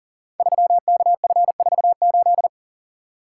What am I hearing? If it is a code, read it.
Morse code, heard as a keyed tone, about 40 wpm, 700 Hz. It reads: 3KF48
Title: Code Groups - 5 Letter:Numbers - 40wpm